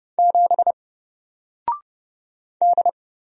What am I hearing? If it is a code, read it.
Morse code, heard as a keyed tone, about 30 wpm, 700 Hz. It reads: 7 D
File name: Single-Letter-Number-30wpm